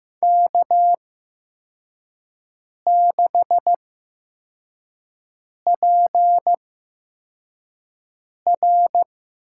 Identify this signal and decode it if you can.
Morse code, heard as a keyed tone, about 15 wpm, spaced as for 4 wpm, 700 Hz. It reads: K6PR
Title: USA Call Signs - 5 Characters - 8x Character Spacing - 15wpm